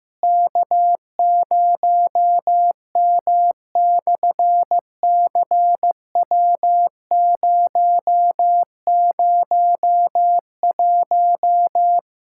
Morse code, heard as a keyed tone, about 16 wpm, 700 Hz. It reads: K0M/CW001
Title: SOTA QSO Practice 15wpm